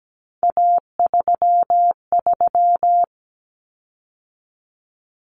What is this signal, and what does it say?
Morse code, heard as a keyed tone, about 17 wpm, 700 Hz. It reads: A33
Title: Common Cut Numbers - 3-Digit 17wpm